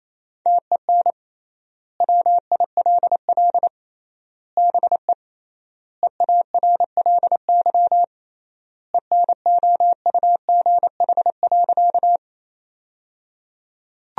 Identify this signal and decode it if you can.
Morse code, heard as a keyed tone, about 28 wpm, 700 Hz. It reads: TEN WILL BE EARLY ENOUGH.
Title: 3x Word-Spacing - Sentences from Top 400 Words - 28wpm